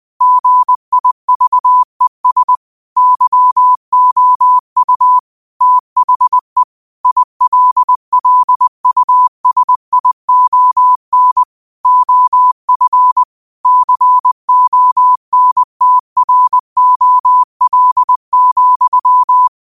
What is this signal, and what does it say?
Morse code, heard as a keyed tone, about 20 wpm, 1000 Hz. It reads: GIVES YOU THE ILLUSION OF CONTROL,